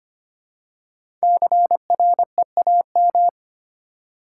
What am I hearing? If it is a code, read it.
Morse code, heard as a keyed tone, about 26 wpm, 700 Hz. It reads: CREAM
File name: 5 Letter Words - First 3 Letters Given - 25wpm